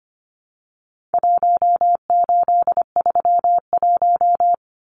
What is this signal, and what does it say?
Morse code, heard as a keyed tone, about 25 wpm, 700 Hz. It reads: 1831
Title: Numbers - Contest Style - 4-Digit 25wpm